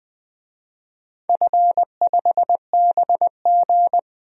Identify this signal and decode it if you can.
Morse code, heard as a keyed tone, about 20 wpm, 700 Hz. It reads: F5BG
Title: DXCC-100 Call Signs - 4 Characters - 20wpm